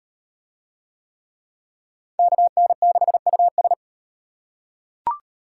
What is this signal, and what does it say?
Morse code, heard as a keyed tone, about 38 wpm, 700 Hz. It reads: KN6US